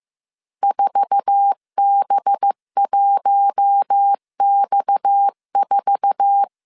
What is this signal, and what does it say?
Morse code, heard as a keyed tone, about 15 wpm, 800 Hz. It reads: 4B1X4
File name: CW / DET / 1 9 1500C S C 14wpm 3245